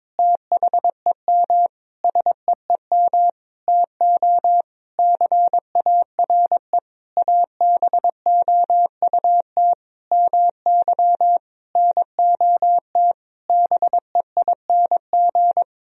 Morse code, heard as a keyed tone, about 22 wpm, 700 Hz. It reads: THEM SEEM TO CARE ABOUT MY NOT BEING
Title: Sentences from Top 1000 Words - Review Part 3 - 22wpm